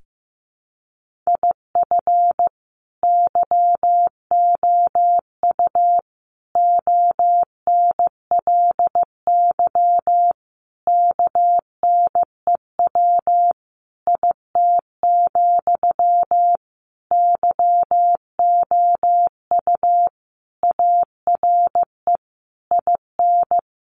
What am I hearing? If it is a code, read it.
Morse code, heard as a keyed tone, about 14 wpm, 700 Hz. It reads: IF YOU ONLY KNEW IT, YOU ARE IN
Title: Sentences from Top 1500 Words 15wpm